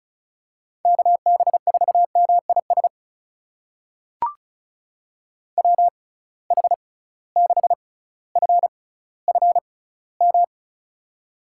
Morse code, heard as a keyed tone, about 36 wpm, spaced as for 18 wpm, 700 Hz. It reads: KB4MIS WH6FFM
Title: USA Call Signs - 6 Characters - 6x Character Spacing - 35wpm